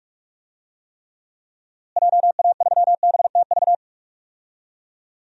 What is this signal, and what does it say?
Morse code, heard as a keyed tone, about 42 wpm, 700 Hz. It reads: JA3BTV
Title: DXCC-100 Call Signs - 6 Characters - Repeated 3x - 45wpm